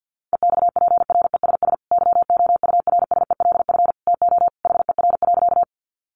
Morse code, heard as a keyed tone, about 50 wpm, 700 Hz. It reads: EXPRESS YOURSELF TO HER.
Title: Sentences from Top 1100 Words 50wpm